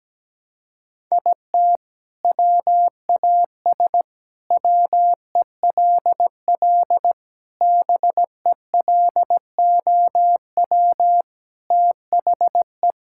Morse code, heard as a keyed tone, about 17 wpm, 700 Hz. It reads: IT WAS WELL BELOW THE
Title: Sentences from Top 1000 Words - Review Part 6 - 17wpm